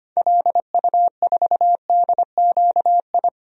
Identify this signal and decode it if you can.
Morse code, heard as a keyed tone, about 25 wpm, 700 Hz. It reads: LU4DQI
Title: DXCC-100 Call Signs - 6 Characters - 25wpm